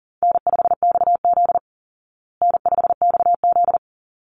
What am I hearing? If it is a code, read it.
Morse code, heard as a keyed tone, about 40 wpm, 700 Hz. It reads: N5XZ N5XZ
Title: Call Signs Repeated 3x - 40wpm